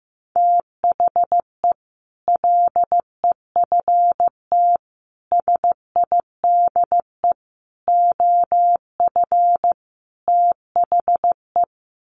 Morse code, heard as a keyed tone, about 15 wpm, 700 Hz. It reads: THE LEFT SIDE OF THE
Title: Sets of 5 Words Encore 15wpm